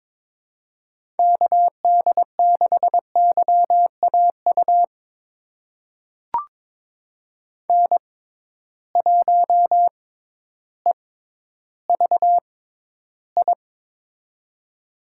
Morse code, heard as a keyed tone, about 22 wpm, spaced as for 12 wpm, 700 Hz. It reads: KD6YAU N1EVI
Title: USA Call Signs - 6x Character Spacing - 22wpm